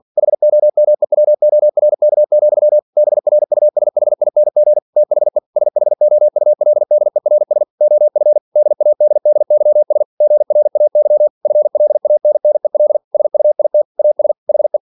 Morse code, heard as a keyed tone, about 46 wpm, 600 Hz. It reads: HOMEWORK, BRUSHING THE SHOULDERS OF DADDYS GRAY FLANNEL SUIT AS HE
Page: Never 47wpm